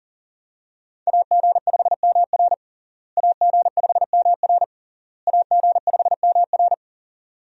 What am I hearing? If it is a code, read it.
Morse code, heard as a keyed tone, about 40 wpm, 700 Hz. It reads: AG5MR AG5MR AG5MR
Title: USA Call Signs - 5 Characters - Repeated 3x - 40wpm